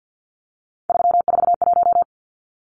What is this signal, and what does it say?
Morse code, heard as a keyed tone, about 50 wpm, 700 Hz. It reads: 341
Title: Numbers - 3-Digit 50wpm